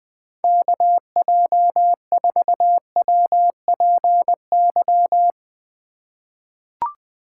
Morse code, heard as a keyed tone, about 20 wpm, 700 Hz. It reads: KJ4WPY
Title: USA Call Signs - 6 Characters - 3x Character Spacing - 20wpm